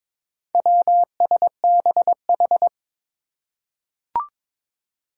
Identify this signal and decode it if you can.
Morse code, heard as a keyed tone, about 22 wpm, 700 Hz. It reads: WSBH
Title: Four Challenging Letter-Number Combinations 22wpm